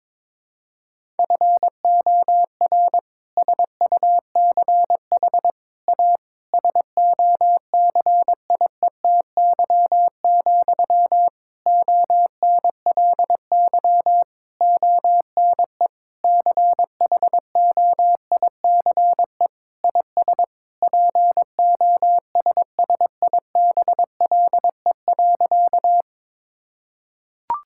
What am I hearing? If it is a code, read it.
Morse code, heard as a keyed tone, about 22 wpm, 700 Hz. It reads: FOR SUCH A SOCIETY, ONLY ONE CHOICE IS POSSIBLE.
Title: Sentences from Top 1000 Words - Review Part 6 - 22wpm